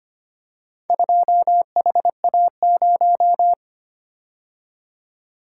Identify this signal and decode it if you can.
Morse code, heard as a keyed tone, about 25 wpm, 700 Hz. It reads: 2HA0
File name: practice / 25wpm / Four Letter-Number Combinations - 25wpm